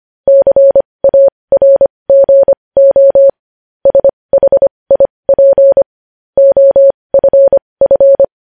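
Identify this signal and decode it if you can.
Morse code, heard as a keyed tone, about 25 wpm, spaced as for 20 wpm, 550 Hz. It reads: CARGO SHIP OFF
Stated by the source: Morse practice sending news headlines at 20 wpm.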